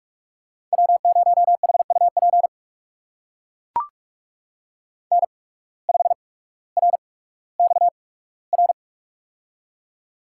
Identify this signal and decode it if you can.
Morse code, heard as a keyed tone, about 44 wpm, spaced as for 19 wpm, 700 Hz. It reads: W0HUP N5RXR
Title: USA Call Signs - 5 Characters - 8x Character Spacing - 45wpm